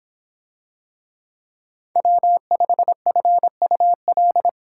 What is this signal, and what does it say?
Morse code, heard as a keyed tone, about 26 wpm, 700 Hz. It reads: W5FUL